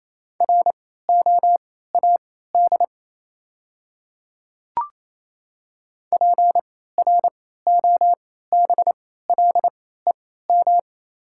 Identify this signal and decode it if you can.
Morse code, heard as a keyed tone, about 28 wpm, spaced as for 16 wpm, 700 Hz. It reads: ROAD PROBLEM
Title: 3x Character Spacing - Top 500 Words - 28wpm